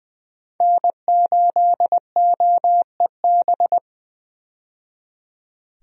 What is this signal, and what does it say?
Morse code, heard as a keyed tone, about 20 wpm, 700 Hz. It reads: N8OEB
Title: USA Call Signs - 5 Characters - 20wpm